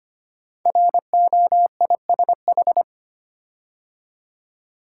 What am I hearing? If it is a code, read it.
Morse code, heard as a keyed tone, about 25 wpm, 700 Hz. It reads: ROISH
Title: Code Groups - 5 Letters - 25wpm